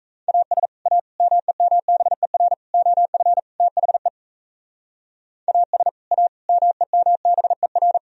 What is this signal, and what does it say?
Morse code, heard as a keyed tone, about 34 wpm, 700 Hz. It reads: AS A MEMBER OF THE AS A MEMBER
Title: Speed-Racing - Sets of 5 Words 28wpm